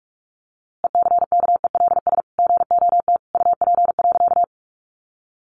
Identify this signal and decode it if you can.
Morse code, heard as a keyed tone, about 44 wpm, 700 Hz. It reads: ECKELS GOT UP.
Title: A Sound Of Thunder 45wpm